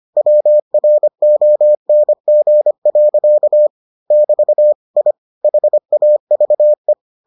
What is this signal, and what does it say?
Morse code, heard as a keyed tone, about 25 wpm, 600 Hz. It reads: WRONG. <BT> I HAVE